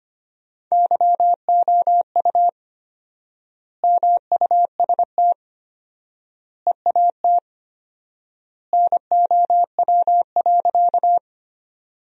Morse code, heard as a keyed tone, about 25 wpm, 700 Hz. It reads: YOU MUST EAT NOW.
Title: 4x Word-Spacing - Sentences from Top 300 Words - 25wpm